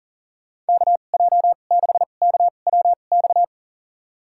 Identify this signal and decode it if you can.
Morse code, heard as a keyed tone, about 40 wpm, spaced as for 29 wpm, 700 Hz. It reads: KJ6KWX
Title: USA Call Signs - 6 Characters - 2x Character Spacing - 40wpm